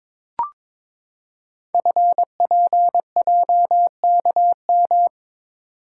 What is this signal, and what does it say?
Morse code, heard as a keyed tone, about 22 wpm, 700 Hz. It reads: FPJKM
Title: Code Groups - 5 Letter:Numbers - 22wpm